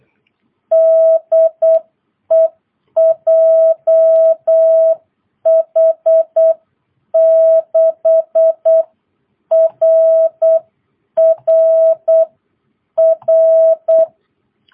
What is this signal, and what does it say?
Morse code, heard as a keyed tone, about 8 wpm, 660 Hz. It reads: DE JH6RRR